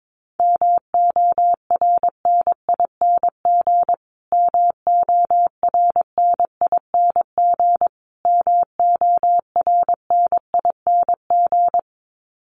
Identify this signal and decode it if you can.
Morse code, heard as a keyed tone, about 22 wpm, 700 Hz. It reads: MORNING MORNING MORNING
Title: Top 500 Words Repeated 22wpm